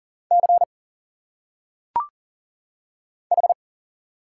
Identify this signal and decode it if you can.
Morse code, heard as a keyed tone, about 40 wpm, 700 Hz. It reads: C H
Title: Single Letters - Phonetic Pronunciation 40wpm